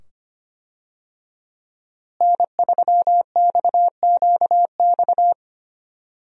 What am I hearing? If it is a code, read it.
Morse code, heard as a keyed tone, about 25 wpm, 700 Hz. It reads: N3XQX
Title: USA Call Signs - 5 Characters - Repeated 2x - 25wpm